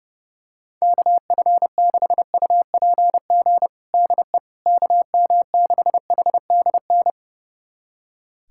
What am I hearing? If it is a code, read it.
Morse code, heard as a keyed tone, about 30 wpm, 700 Hz. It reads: KF6UPG DE KM6HDN
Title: USA Callsign DE Callsign - 6 Characters - 30wpm